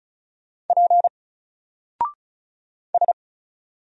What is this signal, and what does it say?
Morse code, heard as a keyed tone, about 34 wpm, 700 Hz. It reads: P S
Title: Single Letter-Number Alt 35wpm